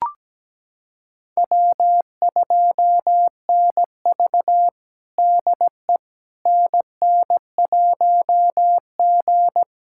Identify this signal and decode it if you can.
Morse code, heard as a keyed tone, about 17 wpm, 700 Hz. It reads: W2NV DE NN1G
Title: USA Callsign DE Callsign - 4 Characters - 17wpm